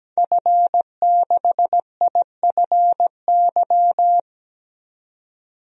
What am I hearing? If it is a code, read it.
Morse code, heard as a keyed tone, about 17 wpm, 700 Hz. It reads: F6IFY
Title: DXCC-100 Call Signs - 5 Characters - 17wpm